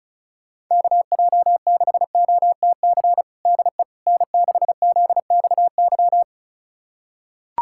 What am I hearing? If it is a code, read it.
Morse code, heard as a keyed tone, about 34 wpm, 700 Hz. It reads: KJ6OTC DE N6ZXY
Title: USA Callsign DE Callsign 35wpm